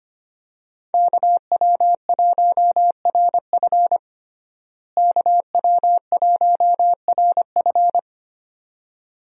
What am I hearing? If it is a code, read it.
Morse code, heard as a keyed tone, about 25 wpm, 700 Hz. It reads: KW1RF KW1RF
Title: USA Call Signs - 5 Characters - Repeated 2x - 25wpm